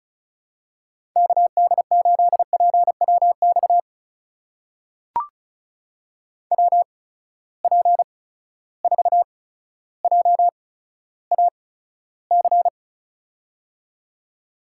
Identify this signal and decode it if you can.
Morse code, heard as a keyed tone, about 36 wpm, spaced as for 15 wpm, 700 Hz. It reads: KD8PWX WP4JAC
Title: USA Call Signs - 6 Characters - 8x Character Spacing - 35wpm